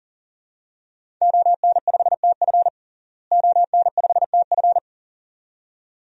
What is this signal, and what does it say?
Morse code, heard as a keyed tone, about 37 wpm, 700 Hz. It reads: ON5TF ON5TF
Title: DXCC-100 Call Signs - 5 Characters - Repeated 2x - 40wpm